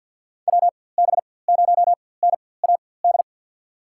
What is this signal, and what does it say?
Morse code, heard as a keyed tone, about 47 wpm, spaced as for 23 wpm, 700 Hz. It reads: WB0NAD
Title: USA Call Signs - 6 Characters - 4x Character Spacing - 50wpm